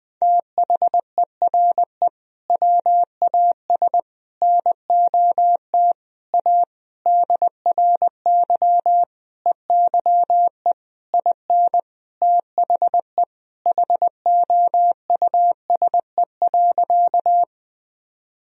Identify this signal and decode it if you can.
Morse code, heard as a keyed tone, about 20 wpm, 700 Hz. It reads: THERE WAS NOT A DRY EYE IN THE HOUSE.
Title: Sentences from Top 1000 Words - Review Part 3 - 20wpm